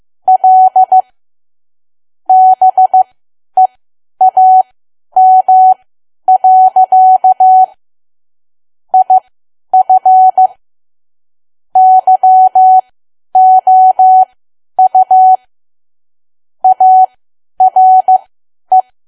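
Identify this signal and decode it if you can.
Morse code, heard as a keyed tone, about 15 wpm, spaced as for 10 wpm, 750 Hz. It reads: L BEAM. IF YOU ARE
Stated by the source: Morse learning practice at 10 wpm.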